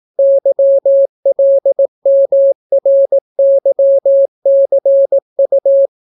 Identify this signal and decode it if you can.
Morse code, heard as a keyed tone, about 18 wpm, 550 Hz. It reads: YLMRYCU